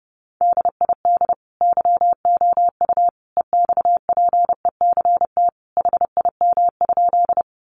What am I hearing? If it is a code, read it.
Morse code, heard as a keyed tone, about 30 wpm, 700 Hz. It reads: DID YOU EXPECT HIM?
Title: Sentences from Top 1100 Words 30wpm